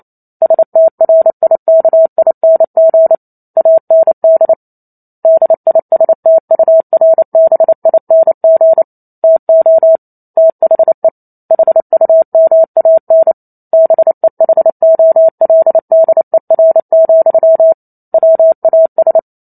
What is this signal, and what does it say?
Morse code, heard as a keyed tone, about 28 wpm, 650 Hz. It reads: STRIKING AND DISTURBING TO THE HUMAN BEHOLDER, WAS
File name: Chapter 2 - 28 WPM